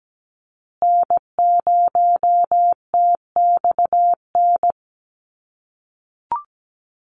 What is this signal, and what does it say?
Morse code, heard as a keyed tone, about 17 wpm, 700 Hz. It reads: N0TXN
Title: USA Call Signs Repeated 2x 17wpm